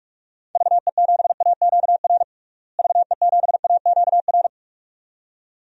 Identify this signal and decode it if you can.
Morse code, heard as a keyed tone, about 46 wpm, 700 Hz. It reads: VE7AQR VE7AQR
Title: DXCC-100 Call Signs - 6 Characters - Repeated 2x - 45wpm